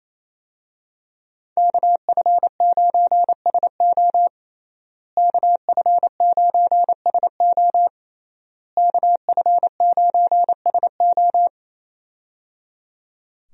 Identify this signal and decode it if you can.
Morse code, heard as a keyed tone, about 28 wpm, 700 Hz. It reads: KF9SO KF9SO KF9SO
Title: USA Call Signs - 5 Characters - Repeated 3x - 28wpm